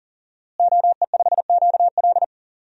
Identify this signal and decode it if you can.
Morse code, heard as a keyed tone, about 40 wpm, 700 Hz. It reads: OE5QL
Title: DXCC-100 Call Signs - 5 Characters - Repeated 2x - 40wpm